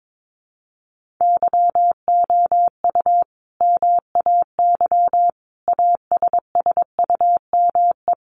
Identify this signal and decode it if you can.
Morse code, heard as a keyed tone, about 22 wpm, 700 Hz. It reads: YOU MAY ASSUME
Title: Sentences from Top 1500 Words 22wpm